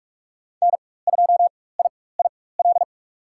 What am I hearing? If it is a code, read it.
Morse code, heard as a keyed tone, about 41 wpm, spaced as for 20 wpm, 700 Hz. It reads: N2IIL
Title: USA Call Signs - 5 Characters - 4x Character Spacing - 45wpm